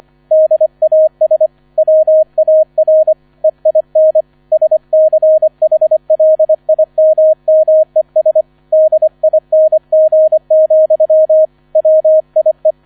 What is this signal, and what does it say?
Morse code, heard as a keyed tone, about 25 wpm, 630 Hz, with radio noise. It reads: DAS WAR EIN SCHLIMMES DING, WIE